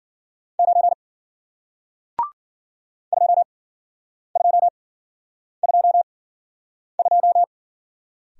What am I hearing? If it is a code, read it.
Morse code, heard as a keyed tone, about 53 wpm, spaced as for 9 wpm, 700 Hz. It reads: 9 2222